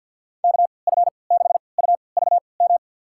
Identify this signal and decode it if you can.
Morse code, heard as a keyed tone, about 51 wpm, spaced as for 28 wpm, 700 Hz. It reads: KF6UVM
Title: USA Call Signs - 6 Characters - 3x Character Spacing - 50wpm